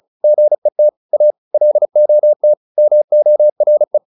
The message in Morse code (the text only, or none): GET A LOT MORE